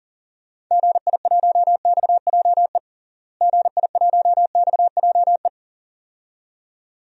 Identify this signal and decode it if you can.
Morse code, heard as a keyed tone, about 40 wpm, 700 Hz. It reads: GI1XJE GI1XJE
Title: DXCC-100 Call Signs - 6 Characters - Repeated 2x - 40wpm